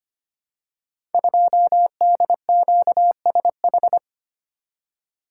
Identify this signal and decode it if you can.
Morse code, heard as a keyed tone, about 25 wpm, 700 Hz. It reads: 2DQSH
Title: Code Groups - 5 Letter:Numbers - 25wpm